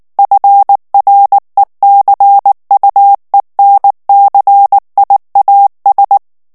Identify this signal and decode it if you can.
Morse code, heard as a keyed tone, about 19 wpm, 800 Hz. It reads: FRECUENCIAS
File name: morse mp3 / ESPAGNOL TEXTE EN CLAIR 2 VITESSE 18 MOTS MINUTES